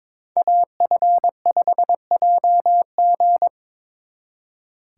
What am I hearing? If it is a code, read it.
Morse code, heard as a keyed tone, about 25 wpm, 700 Hz. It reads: AF5JG
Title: Speed Racing - USA 2x2 Call Signs 22wpm